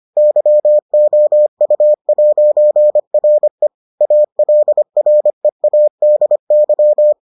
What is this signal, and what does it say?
Morse code, heard as a keyed tone, about 25 wpm, 600 Hz. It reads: YOU'RE ALREADY